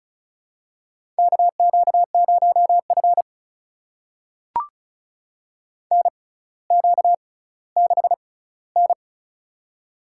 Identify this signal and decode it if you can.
Morse code, heard as a keyed tone, about 34 wpm, spaced as for 18 wpm, 700 Hz. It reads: KQ0F NQ6N
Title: USA Call Signs - 4 Characters - 6x Character Spacing - 35wpm